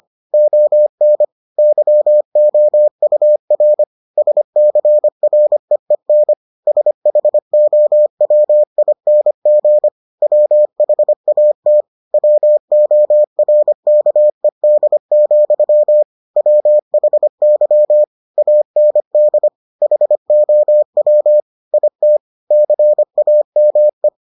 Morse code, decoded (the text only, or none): ON YOUR SCREEN SHOWING WHAT WORKED, WHY AND HOW IT CAME